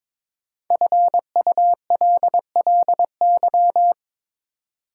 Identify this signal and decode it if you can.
Morse code, heard as a keyed tone, about 23 wpm, 690 Hz. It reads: FULLY
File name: 5 Letter Words - First 3 Letters Given - 22wpm